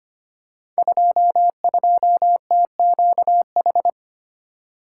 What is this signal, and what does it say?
Morse code, heard as a keyed tone, about 25 wpm, 700 Hz. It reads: 22TQH
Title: Code Groups - 5 Letter:Numbers - 25wpm